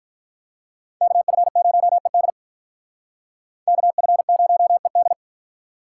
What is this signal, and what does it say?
Morse code, heard as a keyed tone, about 45 wpm, 700 Hz. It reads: KF0ED KF0ED